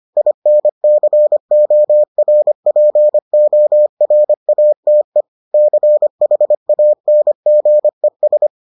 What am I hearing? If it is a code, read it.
Morse code, heard as a keyed tone, about 25 wpm, 600 Hz. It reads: INCORPORATE CHANGES